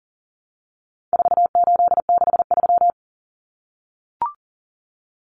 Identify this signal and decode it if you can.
Morse code, heard as a keyed tone, about 40 wpm, 700 Hz. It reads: 4863
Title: Common Cut Numbers - 4-Digit 40wpm